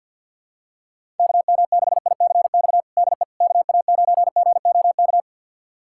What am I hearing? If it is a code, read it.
Morse code, heard as a keyed tone, about 49 wpm, 700 Hz. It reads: KM6ICX DE KA9GOK
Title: USA Callsign DE Callsign - 6 Characters - 50wpm